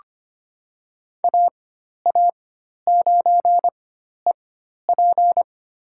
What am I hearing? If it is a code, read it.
Morse code, heard as a keyed tone, about 25 wpm, spaced as for 12 wpm, 700 Hz. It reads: AA9EP
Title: USA Call Signs - 5 Characters - 4x Character Spacing - 25wpm